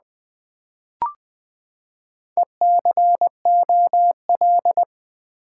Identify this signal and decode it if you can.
Morse code, heard as a keyed tone, about 20 wpm, 700 Hz. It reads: ECOL
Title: Build A Word - 4 Letters At A Time - 10-Letter Words - 20wpm